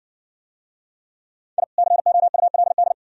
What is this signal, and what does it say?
Morse code, heard as a keyed tone, about 56 wpm, 700 Hz. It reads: I COULD